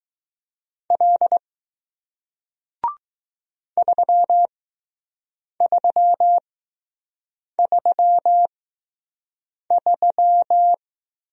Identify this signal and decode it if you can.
Morse code, heard as a keyed tone, about 20 wpm, spaced as for 6 wpm, 700 Hz. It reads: L 3333